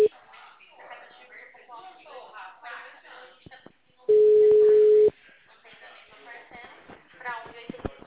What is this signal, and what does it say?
Ringback tone (Europe, 425 Hz), with speech.
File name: out- -2005- - - .167